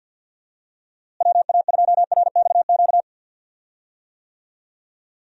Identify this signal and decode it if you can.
Morse code, heard as a keyed tone, about 52 wpm, 700 Hz. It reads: WA2RXQ
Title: USA Call Signs - 6 Characters - Repeated 2x - 50wpm